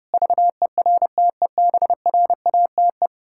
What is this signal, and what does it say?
Morse code, heard as a keyed tone, about 30 wpm, 700 Hz. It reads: VERTEBRATE
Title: Pangram Challenge v2 - 30wpm